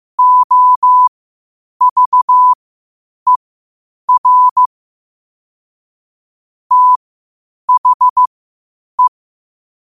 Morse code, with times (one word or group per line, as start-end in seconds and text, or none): 0.2-4.7 OVER
6.7-9.1 THE